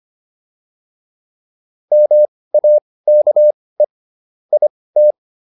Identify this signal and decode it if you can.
Morse code, heard as a keyed tone, about 25 wpm, spaced as for 18 wpm, 600 Hz. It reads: MAKE IT
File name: PR305 18